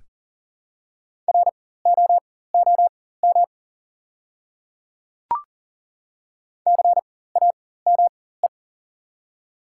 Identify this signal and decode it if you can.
Morse code, heard as a keyed tone, about 40 wpm, spaced as for 19 wpm, 700 Hz. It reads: ROOM CAME
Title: 4x Character Spacing - Top 300 Words - 40wpm